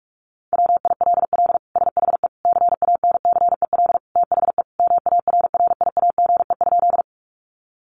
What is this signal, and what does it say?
Morse code, heard as a keyed tone, about 46 wpm, 700 Hz. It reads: WILL SHE CANCEL THE MARRIAGE?
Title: Sentences from Top 1200 Words 45wpm